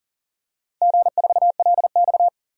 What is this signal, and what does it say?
Morse code, heard as a keyed tone, about 40 wpm, 700 Hz. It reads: G4LX
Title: DXCC-100 Call Signs - 4 Characters - 40wpm